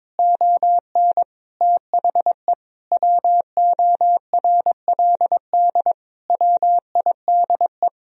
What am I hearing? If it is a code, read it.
Morse code, heard as a keyed tone, about 22 wpm, 700 Hz. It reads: ON THE WORLD WIDE
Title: Sentences from Top 1000 Words - Review Part 4 - 22wpm